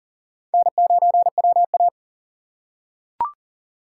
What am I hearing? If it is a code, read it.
Morse code, heard as a keyed tone, about 40 wpm, 700 Hz. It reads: N9WA
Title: USA Call Signs - 4 Characters - 6x Character Spacing - 40wpm